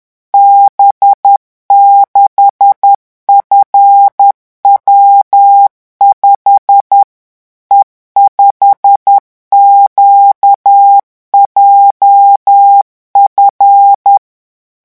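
Morse code, heard as a keyed tone, about 10 wpm, 800 Hz. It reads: B6FW5 E5QJF
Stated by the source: Morse revision exercise for French radio amateurs at 10 words per minute.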